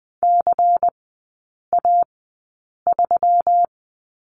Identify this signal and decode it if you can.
Morse code, heard as a keyed tone, about 20 wpm, spaced as for 8 wpm, 700 Hz. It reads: CA3
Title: 152 Lesson 25e - Letter p - Three Characters 20wpm